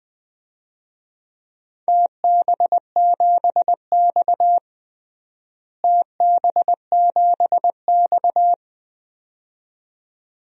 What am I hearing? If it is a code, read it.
Morse code, heard as a keyed tone, about 20 wpm, 700 Hz. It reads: TB7X TB7X
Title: DXCC-100 Call Signs - 4 Characters - Repeated 2x - 20wpm